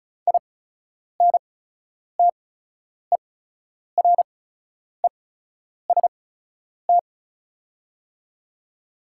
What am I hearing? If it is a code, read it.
Morse code, heard as a keyed tone, about 36 wpm, spaced as for 10 wpm, 690 Hz. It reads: INTEREST
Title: 8x Character Spacing - Top 500 Words - 35wpm